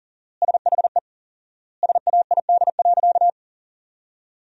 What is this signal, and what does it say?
Morse code, heard as a keyed tone, about 40 wpm, 700 Hz. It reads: SHE SAID.
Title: 4x Word-Spacing - Sentences from Top 100 Words - 40wpm